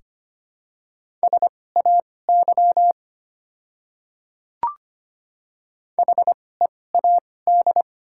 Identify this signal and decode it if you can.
Morse code, heard as a keyed tone, about 25 wpm, spaced as for 18 wpm, 700 Hz. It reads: SAY HEAD
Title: 2x Character Spacing - Top 300 Words - 25wpm